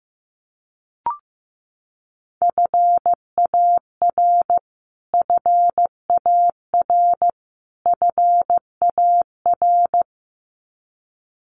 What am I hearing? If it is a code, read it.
Morse code, heard as a keyed tone, about 15 wpm, 700 Hz. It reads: FAR FAR FAR
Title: Top 500 Words Repeated 15wpm